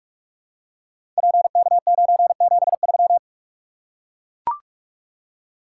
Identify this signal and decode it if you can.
Morse code, heard as a keyed tone, about 44 wpm, 700 Hz. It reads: PK973
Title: Code Groups - 5 Letter:Numbers - 45wpm